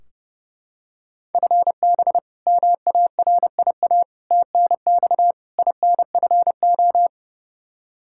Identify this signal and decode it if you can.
Morse code, heard as a keyed tone, about 30 wpm, 700 Hz. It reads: FB MARIA TNX INFO
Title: Practice QSO - Exchange 3 - More Info - 30wpm